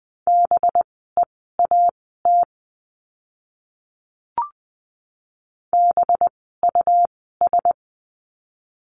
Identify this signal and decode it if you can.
Morse code, heard as a keyed tone, about 20 wpm, spaced as for 14 wpm, 700 Hz. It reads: BEAT BUS
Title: 185 Lesson 30f - Letter b - Words 20wpm